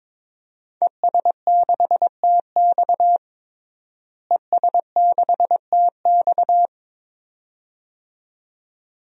Morse code, heard as a keyed tone, about 22 wpm, 700 Hz. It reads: ES6TX ES6TX
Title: DXCC-100 Call Signs - 5 Characters - Repeated 2x - 22wpm